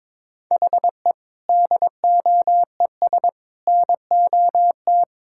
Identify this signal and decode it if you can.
Morse code, heard as a keyed tone, about 21 wpm, 700 Hz. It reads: HE DOES NOT
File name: practice / 22wpm / Sentences with 4-Word Phrases - 22wpm